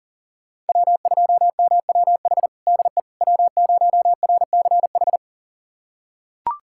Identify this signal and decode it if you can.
Morse code, heard as a keyed tone, about 37 wpm, 700 Hz. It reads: W2MWH DE W0RCH
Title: USA Callsign DE Callsign - 5 Characters - 40wpm